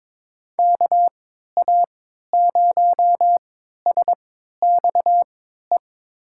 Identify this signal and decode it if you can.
Morse code, heard as a keyed tone, about 21 wpm, spaced as for 12 wpm, 700 Hz. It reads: KA0SXE
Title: USA Call Signs - 6 Characters - 3x Character Spacing - 22wpm